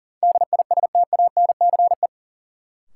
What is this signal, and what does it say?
Morse code, heard as a keyed tone, about 40 wpm, 700 Hz. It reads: DISTANCE
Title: Sentences from Top 1200 Words - New Word Drill Alt - 40wpm